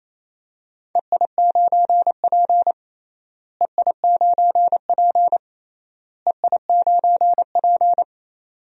Morse code, heard as a keyed tone, about 27 wpm, 700 Hz. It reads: EI9P EI9P EI9P
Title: DXCC-100 Call Signs - 4 Characters - Repeated 3x - 28wpm